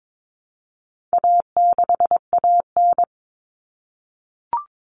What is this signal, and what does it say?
Morse code, heard as a keyed tone, about 22 wpm, 700 Hz. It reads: A6AN
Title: Common Cut Numbers - 4-Digit 22wpm